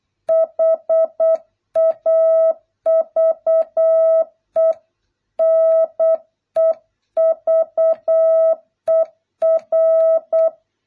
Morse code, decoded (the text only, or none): HAVE NEVER